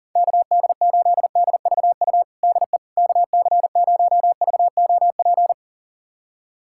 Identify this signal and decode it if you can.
Morse code, heard as a keyed tone, about 40 wpm, 700 Hz. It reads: KD8DVU DE KC0VOP
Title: USA Callsign DE Callsign 40wpm